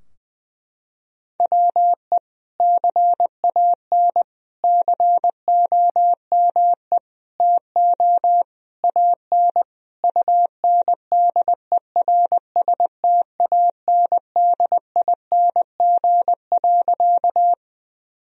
Morse code, heard as a keyed tone, about 20 wpm, 700 Hz. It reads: WE CAN COME TO AN UNDERSTANDING.
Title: Sentences from Top 1000 Words - Review Part 3 - 20wpm